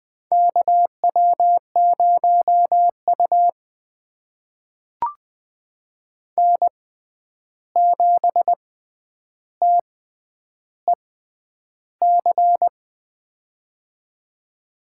Morse code, heard as a keyed tone, about 20 wpm, spaced as for 10 wpm, 700 Hz. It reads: KW0U N7TEC
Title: USA Call Signs - 6x Character Spacing - 20wpm